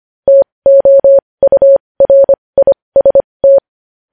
Morse code, heard as a keyed tone, about 25 wpm, spaced as for 20 wpm, 550 Hz. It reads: TOURIST
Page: News Headlines in Morse Code at 20 WPM